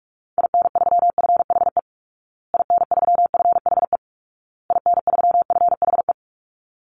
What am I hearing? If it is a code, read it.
Morse code, heard as a keyed tone, about 44 wpm, 700 Hz. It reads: IN3FHE IN3FHE IN3FHE
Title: Call Signs Repeated 2x - 45wpm